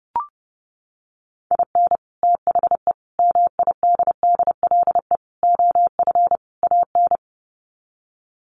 Morse code, heard as a keyed tone, about 30 wpm, 700 Hz. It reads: IN THE MIDDLE OF AN
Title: Sets of 5 Words Encore2 30wpm